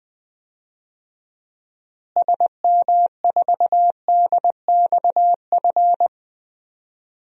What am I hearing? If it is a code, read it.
Morse code, heard as a keyed tone, about 21 wpm, 700 Hz. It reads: SM4DXF
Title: DXCC-100 Call Signs - 6 Characters - Repeated 2x - 20wpm